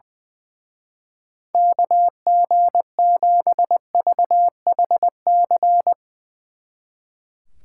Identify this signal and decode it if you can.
Morse code, heard as a keyed tone, about 20 wpm, 700 Hz. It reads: KG7VHC
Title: Speed Racing - USA 2x3 Call Signs 20wpm